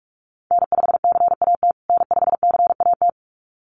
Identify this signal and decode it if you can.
Morse code, heard as a keyed tone, about 44 wpm, 690 Hz. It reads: N5CAT N5CAT
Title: USA Call Signs Repeated 2x 45wpm